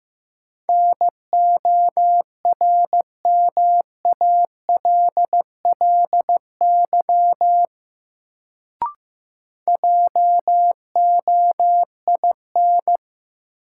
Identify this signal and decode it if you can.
Morse code, heard as a keyed tone, about 15 wpm, 700 Hz. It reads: NORMALLY JOIN
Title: Sentences from Top 1000 Words - Review Part 6 - New Word Drill - 15wpm